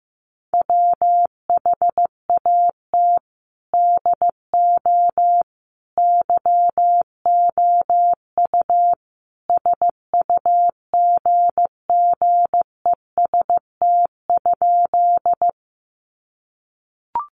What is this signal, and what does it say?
Morse code, heard as a keyed tone, about 15 wpm, 700 Hz. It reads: WHAT DO YOU SUGGEST?
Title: Sentences from Top 1300 Words 15wpm